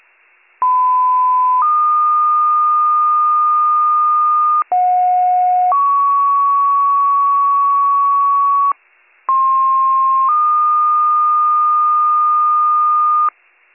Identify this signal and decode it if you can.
Two-tone page, heard as a one-second tone, then a three-second tone, about 1010 then 1220 Hz, about 730 then 1120 Hz, about 1060 then 1220 Hz. It is heard from 0.6 to 13.3 s.